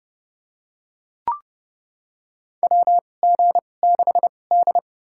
Morse code, heard as a keyed tone, about 30 wpm, spaced as for 22 wpm, 700 Hz. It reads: WG6D
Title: USA Call Signs - 4 Characters - 2x Character Spacing - 30wpm